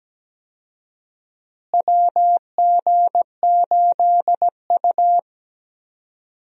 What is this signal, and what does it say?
Morse code, heard as a keyed tone, about 17 wpm, 700 Hz. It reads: WG8U
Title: USA Call Signs - 4 Characters - Repeated 2x - 17wpm